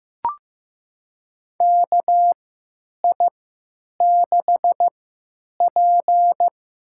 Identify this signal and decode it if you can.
Morse code, heard as a keyed tone, about 15 wpm, spaced as for 9 wpm, 700 Hz. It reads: KI6P
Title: USA Call Signs - 3x Character Spacing - 15wpm